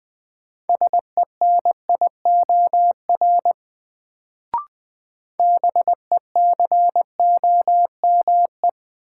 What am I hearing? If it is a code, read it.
Morse code, heard as a keyed tone, about 20 wpm, 700 Hz. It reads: SENIOR BECOME
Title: Sets of 5 Words Encore - New Word Drill - 20wpm